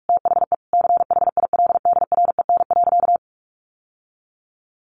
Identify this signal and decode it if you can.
Morse code, heard as a keyed tone, about 44 wpm, 700 Hz. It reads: THE CHILDREN.
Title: Sentences from Top 1100 Words 45wpm